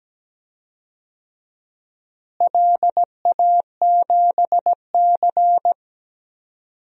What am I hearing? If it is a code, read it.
Morse code, heard as a keyed tone, about 17 wpm, 700 Hz. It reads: LA7C
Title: DXCC-100 Call Signs - 4 Characters - Repeated 2x - 17wpm